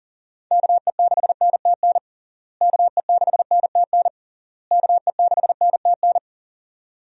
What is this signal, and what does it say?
Morse code, heard as a keyed tone, about 40 wpm, 700 Hz. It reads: KE6NTN KE6NTN KE6NTN
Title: USA Call Signs - 6 Characters - Repeated 3x - 40wpm